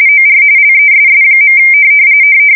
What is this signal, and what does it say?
RTTY, about 45.45 baud. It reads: OR LESS.
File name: fsk170 / teletype-RTTY